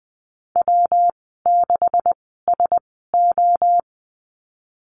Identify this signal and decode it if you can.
Morse code, heard as a keyed tone, about 20 wpm, spaced as for 14 wpm, 700 Hz. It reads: W6SO
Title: 161 Lesson 26g - Letter g - Call Signs 20wpm